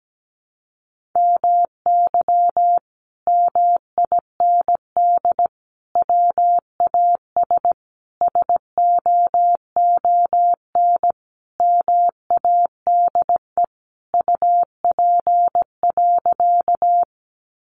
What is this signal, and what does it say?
Morse code, heard as a keyed tone, about 17 wpm, 700 Hz. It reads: MY MIND WAS SOON MADE UP.
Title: Sentences from Top 500 Words 17wpm